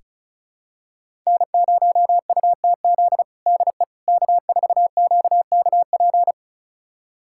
Morse code, heard as a keyed tone, about 36 wpm, 700 Hz. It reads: N0UTZ DE K4QKP
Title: USA Callsign DE Callsign - 5 Characters - 35wpm